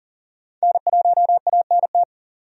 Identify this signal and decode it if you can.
Morse code, heard as a keyed tone, about 40 wpm, 700 Hz. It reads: N1ANT
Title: USA Call Signs - 5 Characters - 40wpm